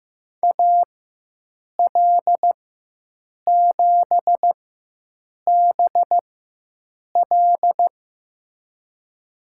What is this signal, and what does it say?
Morse code, heard as a keyed tone, about 16 wpm, spaced as for 7 wpm, 700 Hz. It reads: AL7BL
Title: USA Call Signs - 5 Characters - 4x Character Spacing - 15wpm